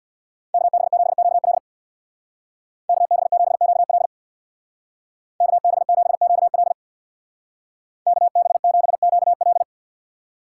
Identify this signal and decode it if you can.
Morse code, heard as a keyed tone, about 62 wpm, 700 Hz. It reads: KB7QL KB7QL KB7QL KB7QL
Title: Speed Racing - USA 2x2 Call Signs 50wpm